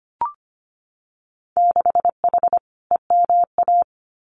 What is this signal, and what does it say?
Morse code, heard as a keyed tone, about 25 wpm, 700 Hz. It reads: 6H EMA
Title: Winter Field Day Exchanges Encore 25wpm